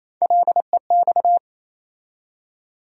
Morse code, heard as a keyed tone, about 28 wpm, 700 Hz. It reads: LEX
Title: USA Call Sign Postfixes - 3 Letters 28wpm